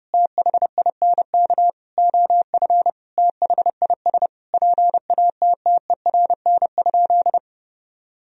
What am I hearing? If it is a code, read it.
Morse code, heard as a keyed tone, about 30 wpm, 700 Hz. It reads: THINK OF THIS PATTERN?
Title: Sentences from Top 1000 Words - Review Part 2 - 30wpm